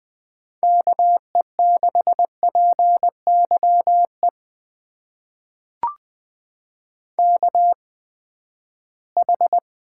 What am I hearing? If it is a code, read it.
Morse code, heard as a keyed tone, about 20 wpm, spaced as for 14 wpm, 700 Hz. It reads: KE6PYE KH
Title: USA Call Signs - 8x Character Spacing - 20wpm